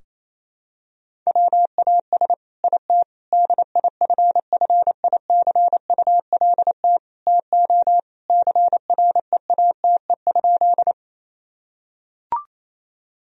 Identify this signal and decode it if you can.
Morse code, heard as a keyed tone, about 28 wpm, 700 Hz. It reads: WAS IT DIFFICULT TO CREATE?
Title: Sentences from Top 1000 Words - Review Part 3 - 28wpm